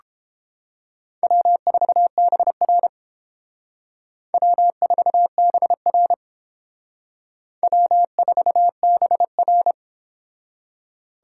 Morse code, heard as a keyed tone, about 28 wpm, 700 Hz. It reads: W4BR W4BR W4BR